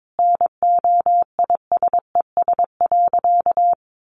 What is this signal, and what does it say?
Morse code, heard as a keyed tone, about 22 wpm, 700 Hz. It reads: NOISES.
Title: Additional Easy Phrases 22wpm